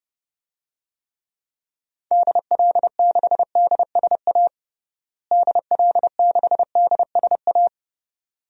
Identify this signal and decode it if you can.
Morse code, heard as a keyed tone, about 32 wpm, 700 Hz. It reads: DL6DSA DL6DSA
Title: DXCC-100 Call Signs - 6 Characters - Repeated 2x - 30wpm